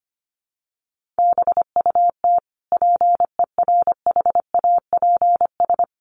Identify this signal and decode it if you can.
Morse code, heard as a keyed tone, about 25 wpm, 700 Hz. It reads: BUT PERHAPS
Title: Sentences from Top 1400 Words 25wpm